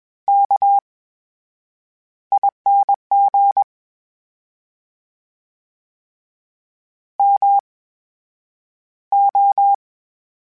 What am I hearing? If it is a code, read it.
Morse code, heard as a keyed tone, about 21 wpm, spaced as for 8 wpm, 800 Hz. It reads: KING MO
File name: Special INGs-mp3s / Special INGs-mp3s / VerySlowToFaster Ing @20